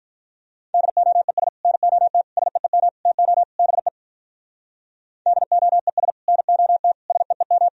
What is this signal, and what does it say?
Morse code, heard as a keyed tone, about 44 wpm, 700 Hz. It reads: DOES NOT SEEM TO BE DOES NOT SEEM